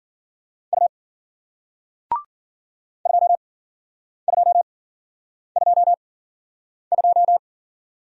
Morse code, heard as a keyed tone, about 52 wpm, spaced as for 9 wpm, 700 Hz. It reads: U 2222